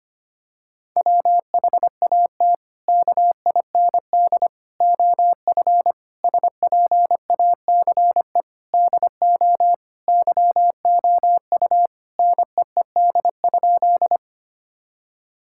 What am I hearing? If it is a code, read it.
Morse code, heard as a keyed tone, about 25 wpm, 700 Hz. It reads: WHAT KIND OF SPACE DO YOU NEED?
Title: Sentences from Top 1000 Words - Review Part 5 - 25wpm